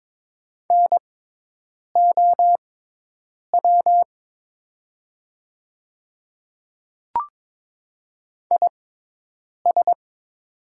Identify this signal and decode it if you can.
Morse code, heard as a keyed tone, about 22 wpm, spaced as for 8 wpm, 700 Hz. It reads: NOW IS
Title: 6x Character Spacing - Top 100 Words - 22wpm